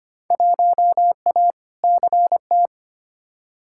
Morse code, heard as a keyed tone, about 25 wpm, 700 Hz. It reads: 1A CT
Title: Field Day Exchanges Encore - 25wpm